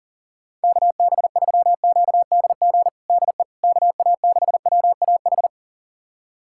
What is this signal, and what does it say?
Morse code, heard as a keyed tone, about 40 wpm, 690 Hz. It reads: KB3QDG DE KA6WAH
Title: USA Callsign DE Callsign - 6 Characters - 40wpm